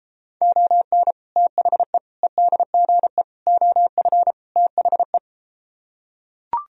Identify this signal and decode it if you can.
Morse code, heard as a keyed tone, about 33 wpm, 700 Hz. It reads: ON THE EDGE OF THE